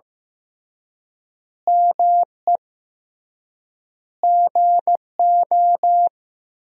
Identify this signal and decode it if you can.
Morse code, heard as a keyed tone, about 16 wpm, 700 Hz. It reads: ME GO
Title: 3x Word-Spacing - Sentences from Top 300 Words - 15wpm